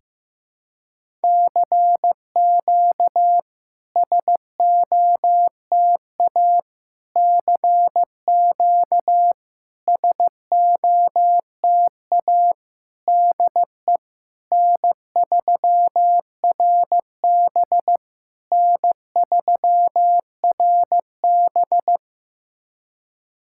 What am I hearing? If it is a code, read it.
Morse code, heard as a keyed tone, about 15 wpm, 700 Hz. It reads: CQ SOTA CQ SOTA DE N3RB N3RB
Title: SOTA QSO Practice 15wpm